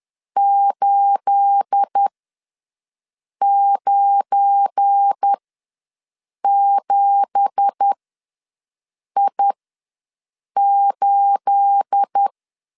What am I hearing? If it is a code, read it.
Morse code, heard as a keyed tone, about 10 wpm, spaced as for 5 wpm, 800 Hz. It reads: 897I8
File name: LEZIONI / 9 2867 ws 10wpm 4215